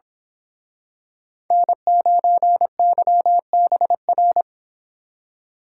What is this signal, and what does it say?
Morse code, heard as a keyed tone, about 26 wpm, 700 Hz. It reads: N9YBR
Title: Speed Racing - USA 1x3 Call Signs 22wpm